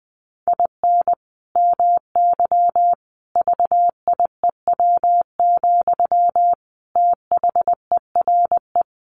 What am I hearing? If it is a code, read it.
Morse code, heard as a keyed tone, about 20 wpm, 700 Hz. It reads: IN MY VIEW, THERE
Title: Sentences from Top 1500 Words 20wpm